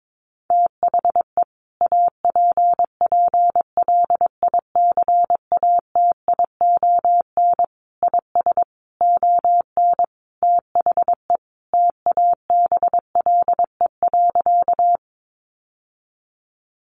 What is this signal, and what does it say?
Morse code, heard as a keyed tone, about 22 wpm, 700 Hz. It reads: THE APPLICATION IS ON THE TABLE.
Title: Sentences from Top 1200 Words 22wpm